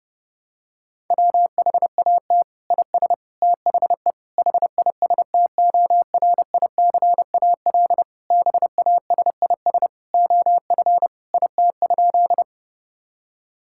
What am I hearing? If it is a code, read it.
Morse code, heard as a keyed tone, about 30 wpm, 700 Hz. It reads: WHAT IS THE HISTORICAL BASIS OF IT?
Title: Sentences from Top 1000 Words - Review Part 5 - 30wpm